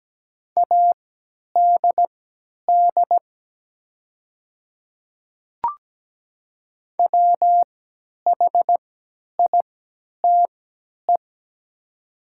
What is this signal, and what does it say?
Morse code, heard as a keyed tone, about 17 wpm, spaced as for 10 wpm, 700 Hz. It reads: ADD WHITE
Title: 3x Character Spacing - Top 300 Words - 17wpm